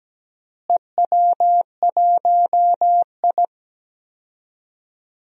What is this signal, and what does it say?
Morse code, heard as a keyed tone, about 17 wpm, 700 Hz. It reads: EW1I
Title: DXCC-100 Call Signs - 4 Characters - Repeated 3x - 17wpm